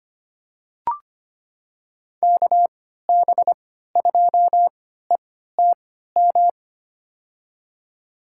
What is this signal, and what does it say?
Morse code, heard as a keyed tone, about 25 wpm, spaced as for 14 wpm, 700 Hz. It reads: KB2ETM
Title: USA Call Signs - 3x Character Spacing - 25wpm